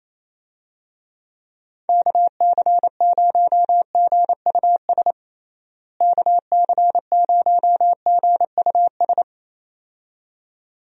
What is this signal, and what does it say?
Morse code, heard as a keyed tone, about 28 wpm, 700 Hz. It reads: KC0GUS KC0GUS
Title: USA Call Signs - 6 Characters - Repeated 2x - 28wpm